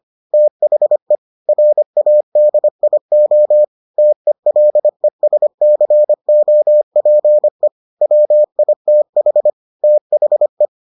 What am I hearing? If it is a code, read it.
Morse code, heard as a keyed tone, about 25 wpm, 600 Hz. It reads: THE RADIO TELESCOPE WITH THE